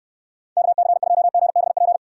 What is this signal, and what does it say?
Morse code, heard as a keyed tone, about 69 wpm, 700 Hz. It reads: KC2GBP